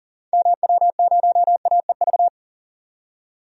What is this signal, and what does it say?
Morse code, heard as a keyed tone, about 40 wpm, 700 Hz. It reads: MW0AEV
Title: DXCC-100 Call Signs - 6 Characters - 40wpm